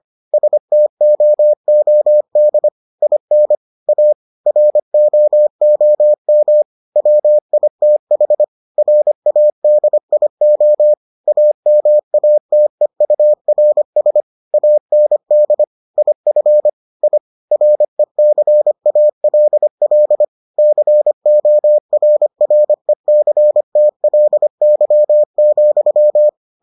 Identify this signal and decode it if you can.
Morse code, heard as a keyed tone, about 25 wpm, 600 Hz. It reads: STOOD IN A ROOM WITH RADIO AMATEURS AND IF I RECALL CORRECTLY,